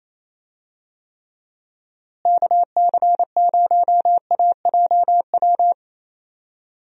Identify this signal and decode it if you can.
Morse code, heard as a keyed tone, about 28 wpm, 700 Hz. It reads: KC0AJW
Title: USA Call Signs - 6 Characters - Repeated 3x - 28wpm